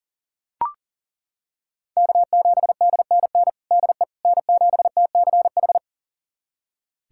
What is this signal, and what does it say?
Morse code, heard as a keyed tone, about 40 wpm, 700 Hz. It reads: K7DNN DE N7TCH
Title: USA Callsign DE Callsign - 5 Characters - 40wpm